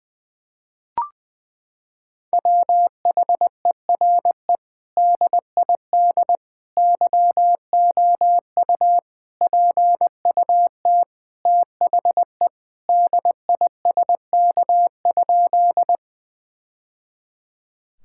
Morse code, heard as a keyed tone, about 21 wpm, 700 Hz. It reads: WHERE DID YOU PUT THE DISK?
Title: Sentences from Top 1000 Words - Review Part 6 - 20wpm